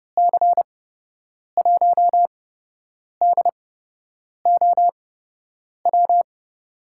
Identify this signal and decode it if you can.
Morse code, heard as a keyed tone, about 30 wpm, spaced as for 8 wpm, 700 Hz. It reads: C1DOW
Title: USA Call Signs - 8x Character Spacing - 30wpm